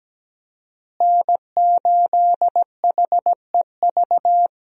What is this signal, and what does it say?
Morse code, heard as a keyed tone, about 17 wpm, 700 Hz. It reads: N8HEV
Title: USA Call Signs - 5 Characters - Repeated 3x - 17wpm